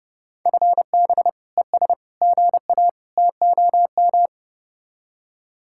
Morse code, heard as a keyed tone, about 31 wpm, 700 Hz. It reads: FB ES GA TOM